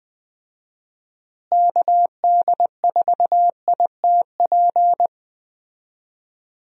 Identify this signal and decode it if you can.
Morse code, heard as a keyed tone, about 20 wpm, 700 Hz. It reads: KD4ITP
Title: USA Call Signs - 6 Characters - Repeated 2x - 20wpm